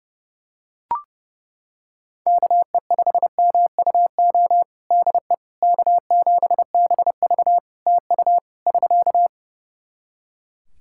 Morse code, heard as a keyed tone, about 28 wpm, 700 Hz. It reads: KE5MUO DE K7BV TU <SK>
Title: K1USN SST QSO Practice 30wpm